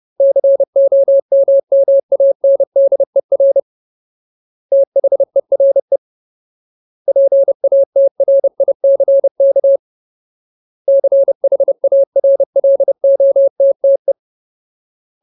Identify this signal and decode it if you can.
Morse code, heard as a keyed tone, about 30 wpm, 550 Hz. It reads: COMMANDER THERE PATRICK CHARLOTTE
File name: Ultimate Word List / 30WPM Ultimate